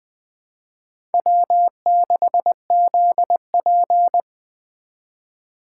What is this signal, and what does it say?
Morse code, heard as a keyed tone, about 24 wpm, 700 Hz. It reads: W6ZP
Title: Speed Racing - USA Call Signs 20wpm